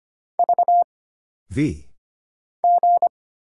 Morse code, heard as a keyed tone, about 25 wpm, 700 Hz. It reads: V G
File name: Single Character (Letter:Number:Punctuation:Prosign) - Mind-Melt - 25wpm